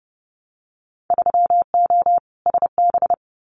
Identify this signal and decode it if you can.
Morse code, heard as a keyed tone, about 32 wpm, 700 Hz. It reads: 3O SB
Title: Winter Field Day Exchanges Encore 30wpm